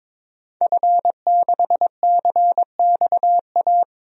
Morse code, heard as a keyed tone, about 22 wpm, 700 Hz. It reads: F6CXA
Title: DXCC-100 Call Signs - 5 Characters - Repeated 3x - 22wpm